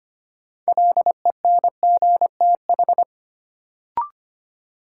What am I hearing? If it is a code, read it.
Morse code, heard as a keyed tone, about 25 wpm, 700 Hz. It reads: LENGTH
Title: Sentences from Top 1000 Words - New Word Drill - 25wpm